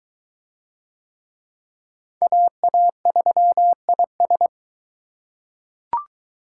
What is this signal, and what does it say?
Morse code, heard as a keyed tone, about 22 wpm, 700 Hz. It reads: AA3IS